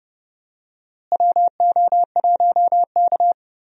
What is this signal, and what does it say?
Morse code, heard as a keyed tone, about 30 wpm, 700 Hz. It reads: WO1K
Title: USA Call Signs - 4 Characters - Repeated 2x - 30wpm